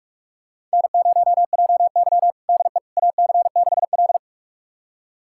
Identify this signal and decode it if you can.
Morse code, heard as a keyed tone, about 46 wpm, 690 Hz. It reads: N0JY DE AC6L
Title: USA Callsign DE Callsign - 4 Characters - 45wpm